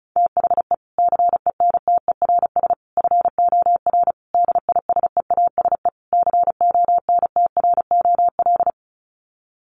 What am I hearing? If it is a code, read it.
Morse code, heard as a keyed tone, about 34 wpm, 700 Hz. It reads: THE CENTERS FOR DISEASE CONTROL
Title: Sets of 5 Words Encore2 35wpm